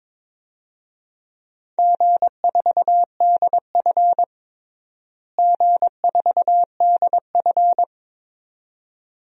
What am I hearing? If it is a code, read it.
Morse code, heard as a keyed tone, about 22 wpm, 700 Hz. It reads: G4DF G4DF
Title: DXCC-100 Call Signs - 4 Characters - Repeated 2x - 22wpm